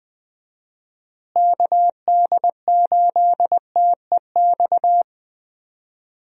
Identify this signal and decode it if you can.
Morse code, heard as a keyed tone, about 20 wpm, 700 Hz. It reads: KD8TEX
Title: USA Call Signs - 6 Characters - Repeated 3x - 20wpm